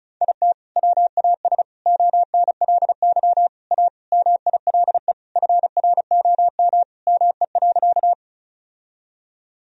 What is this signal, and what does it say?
Morse code, heard as a keyed tone, about 34 wpm, 700 Hz. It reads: IT WAS ONLY A MILE FROM ME.
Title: Sentences from Top 1000 Words - Review Part 1 - 35wpm